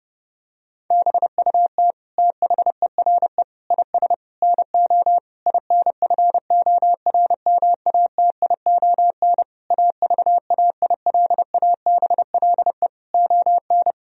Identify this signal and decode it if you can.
Morse code, heard as a keyed tone, about 30 wpm, 700 Hz. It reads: BUT THERE IS NO INFORMATION AVAILABLE ON
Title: Sentences from Top 1000 Words - Review Part 3 - 30wpm